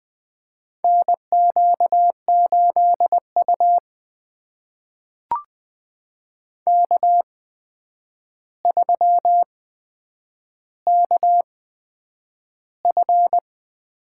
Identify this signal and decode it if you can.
Morse code, heard as a keyed tone, about 20 wpm, spaced as for 9 wpm, 700 Hz. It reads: NQ8U K3KF
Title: USA Call Signs - 4 Characters - 8x Character Spacing - 20wpm